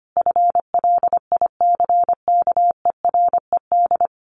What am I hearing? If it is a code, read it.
Morse code, heard as a keyed tone, about 25 wpm, 700 Hz. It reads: FLICKERED
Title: A Sound Of Thunder 25wpm